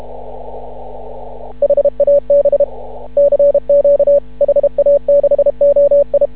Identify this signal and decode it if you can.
Morse code, heard as a keyed tone, about 31 wpm, 560 Hz. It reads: HAB CQ HA6OI
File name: CW / ha6oi 2